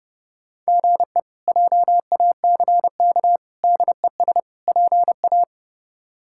Morse code, heard as a keyed tone, about 30 wpm, 700 Hz. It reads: GE JACK DES PA
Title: K1USN SST QSO Practice - 30wpm